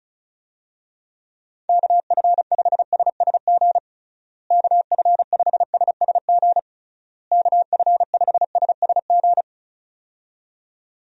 Morse code, heard as a keyed tone, about 34 wpm, 700 Hz. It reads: KF5SSG KF5SSG KF5SSG
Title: USA Call Signs - 6 Characters - Repeated 3x - 35wpm